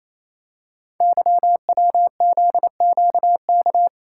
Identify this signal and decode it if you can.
Morse code, heard as a keyed tone, about 27 wpm, 700 Hz. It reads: YWZQK
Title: Code Groups - 5 Letters - 28wpm